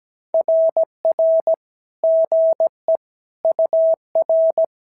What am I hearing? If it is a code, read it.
Morse code, heard as a keyed tone, about 16 wpm, 650 Hz. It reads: RR GE UR
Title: SOTA QSO Practice 17wpm